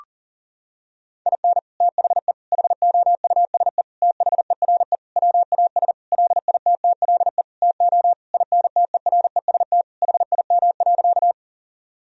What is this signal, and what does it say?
Morse code, heard as a keyed tone, about 40 wpm, 700 Hz. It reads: IN THE HOUSE THERE WAS LITTLE TO INTEREST HIM.
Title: Sentences from Top 1000 Words - Review Part 2 - 40wpm